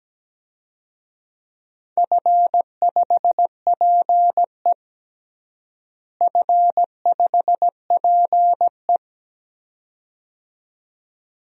Morse code, heard as a keyed tone, about 17 wpm, 700 Hz. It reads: F5PE F5PE
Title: DXCC-100 Call Signs - 4 Characters - Repeated 2x - 17wpm